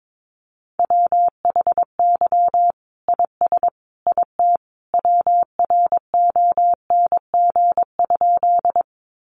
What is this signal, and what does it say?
Morse code, heard as a keyed tone, about 22 wpm, 700 Hz. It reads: WHY IS IT WRONG?
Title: Sentences from Top 900 Words 22wpm